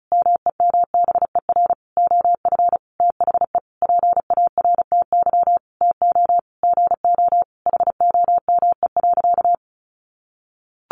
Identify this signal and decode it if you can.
Morse code, heard as a keyed tone, about 34 wpm, 700 Hz. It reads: MEMBER OF THE PARTY TO GO HOME.
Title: Sentences from Top 1400 Words 35wpm